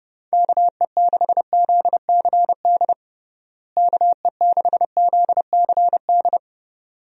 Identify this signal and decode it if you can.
Morse code, heard as a keyed tone, about 32 wpm, 700 Hz. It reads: KE6ZCD KE6ZCD
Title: USA Call Signs - 6 Characters - Repeated 2x - 30wpm